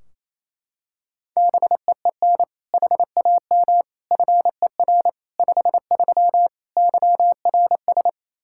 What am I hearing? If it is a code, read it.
Morse code, heard as a keyed tone, about 27 wpm, 700 Hz. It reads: BEEN HAM FER 53 YRS
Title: Practice QSO - Exchange 3 - More Info - 28wpm